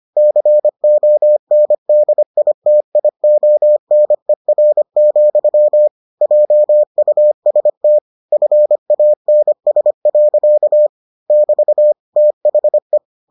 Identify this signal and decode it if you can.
Morse code, heard as a keyed tone, about 25 wpm, 600 Hz. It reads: CONDITIONER, JUST FANS. <BT> THE